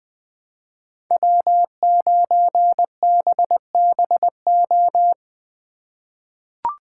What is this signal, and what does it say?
Morse code, heard as a keyed tone, about 20 wpm, 700 Hz. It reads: W9BBO
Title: USA Call Signs - 5 Characters - 4x Character Spacing - 20wpm